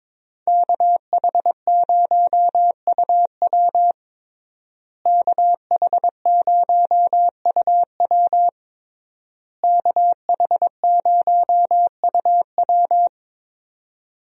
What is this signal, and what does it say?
Morse code, heard as a keyed tone, about 22 wpm, 700 Hz. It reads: KH0UW KH0UW KH0UW
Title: USA Call Signs - 5 Characters - Repeated 3x - 22wpm